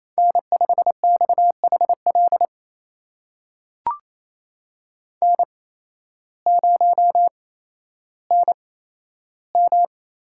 Morse code, heard as a keyed tone, about 27 wpm, spaced as for 13 wpm, 700 Hz. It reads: N5XHL N0NM
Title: USA Call Signs - 5 Characters - 8x Character Spacing - 28wpm